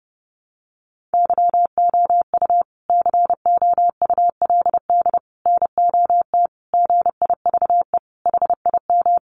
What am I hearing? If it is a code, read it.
Morse code, heard as a keyed tone, about 30 wpm, 700 Hz. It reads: YOU COULD NOT GIVE HIM
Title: Sentences from Top 1500 Words 30wpm